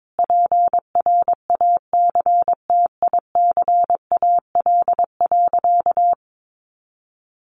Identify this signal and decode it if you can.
Morse code, heard as a keyed tone, about 22 wpm, 700 Hz. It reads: PRACTICAL.
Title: Sentences from Top 1400 Words 22wpm